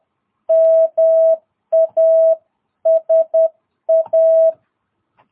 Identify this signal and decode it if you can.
Morse code, heard as a keyed tone, about 11 wpm, 660 Hz. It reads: MASA
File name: jh8cbh / 2022cw